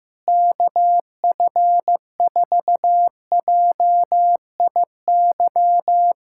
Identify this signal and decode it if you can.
Morse code, heard as a keyed tone, about 16 wpm, 700 Hz. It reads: KF4JIY
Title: USA Callsign DE Callsign - 6 Characters - 15wpm